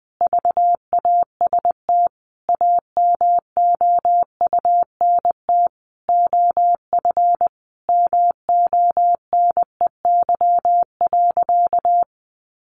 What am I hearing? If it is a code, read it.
Morse code, heard as a keyed tone, about 20 wpm, 700 Hz. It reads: VAST AMOUNT OF MONEY.
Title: Sentences from Top 1300 Words 20wpm